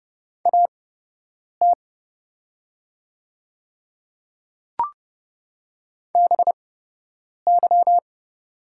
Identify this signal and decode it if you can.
Morse code, heard as a keyed tone, about 30 wpm, spaced as for 8 wpm, 700 Hz. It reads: AT BY
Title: 8x Character Spacing - Top 300 Words - 30wpm